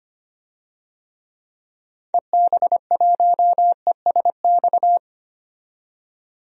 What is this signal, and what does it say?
Morse code, heard as a keyed tone, about 25 wpm, 700 Hz. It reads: EB1ESX
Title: DXCC-100 Call Signs - 6 Characters - 25wpm